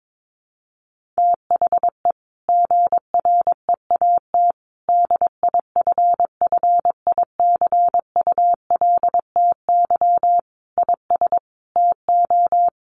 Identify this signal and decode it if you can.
Morse code, heard as a keyed tone, about 22 wpm, 700 Hz. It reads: THE GREAT DIFFICULTY IS TO
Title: Sentences from Top 1500 Words 22wpm